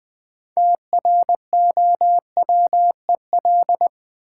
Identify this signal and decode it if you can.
Morse code, heard as a keyed tone, about 20 wpm, 700 Hz. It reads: TROWEL
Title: Easy Words with Hints - 20wpm